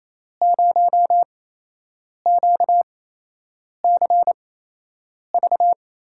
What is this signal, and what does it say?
Morse code, heard as a keyed tone, about 28 wpm, spaced as for 8 wpm, 700 Hz. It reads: 0QCV
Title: USA Call Signs - 8x Character Spacing - 28wpm